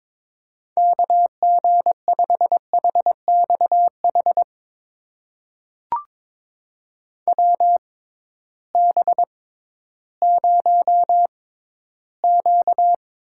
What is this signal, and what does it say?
Morse code, heard as a keyed tone, about 22 wpm, spaced as for 13 wpm, 700 Hz. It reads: KG5HXH WB0Q
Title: USA Call Signs - 6 Characters - 6x Character Spacing - 22wpm